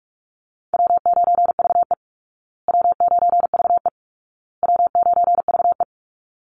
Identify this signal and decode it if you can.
Morse code, heard as a keyed tone, about 44 wpm, 700 Hz. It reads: W9VE W9VE W9VE
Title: Easy Call Signs Repeated 3x - 45wpm